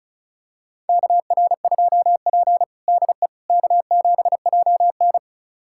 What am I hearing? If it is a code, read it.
Morse code, heard as a keyed tone, about 36 wpm, 700 Hz. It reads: KR2P DE K7JN
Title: USA Callsign DE Callsign - 4 Characters - 35wpm